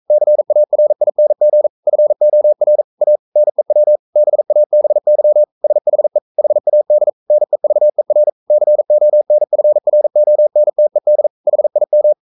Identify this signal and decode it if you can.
Morse code, heard as a keyed tone, about 42 wpm, 600 Hz. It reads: CARING FOR A NEW BABY SHE HAD NEVER CONFRONTED HIM
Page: Romanen Never af Ken Follet på engelsk.